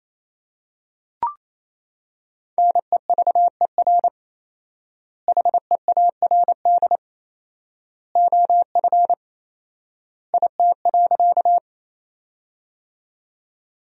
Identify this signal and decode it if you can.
Morse code, heard as a keyed tone, about 27 wpm, 700 Hz. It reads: NEVER HEARD OF IT.
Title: 4x Word-Spacing - Sentences from Top 400 Words - 28wpm